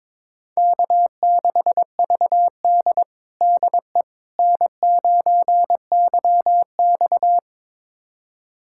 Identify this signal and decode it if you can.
Morse code, heard as a keyed tone, about 21 wpm, 700 Hz. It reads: K6VD DE N9YX
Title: USA Callsign DE Callsign - 4 Characters - 22wpm